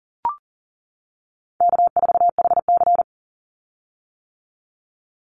Morse code, heard as a keyed tone, about 39 wpm, 700 Hz. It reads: K4HC
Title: Easy Call Signs - 40wpm